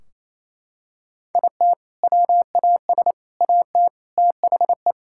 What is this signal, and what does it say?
Morse code, heard as a keyed tone, about 28 wpm, 700 Hz. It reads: IT WAS AT THE
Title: Sentences from Top 1600 Words 28wpm